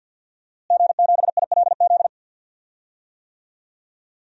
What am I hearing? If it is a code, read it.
Morse code, heard as a keyed tone, about 47 wpm, 700 Hz. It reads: G7ILZ
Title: DXCC-100 Call Signs - 5 Characters - Repeated 3x - 50wpm